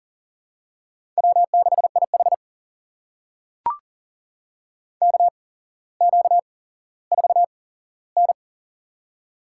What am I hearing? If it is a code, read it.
Morse code, heard as a keyed tone, about 40 wpm, spaced as for 17 wpm, 700 Hz. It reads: W6IH KQ4N
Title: USA Call Signs - 4 Characters - 8x Character Spacing - 40wpm